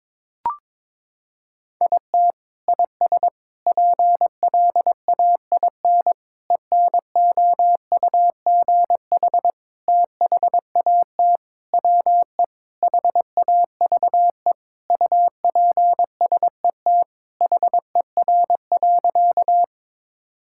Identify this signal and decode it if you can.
Morse code, heard as a keyed tone, about 22 wpm, 700 Hz. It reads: IT IS PLAIN ENOUGH THAT WE HAVE UPSET HER.
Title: Sentences from Top 1000 Words - Review Part 5 - 22wpm